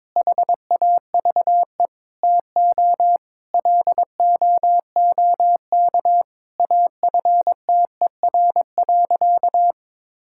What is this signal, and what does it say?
Morse code, heard as a keyed tone, about 22 wpm, 700 Hz. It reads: HAVE TO LOOK AFTER.
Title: Sentences from Top 1600 Words 22wpm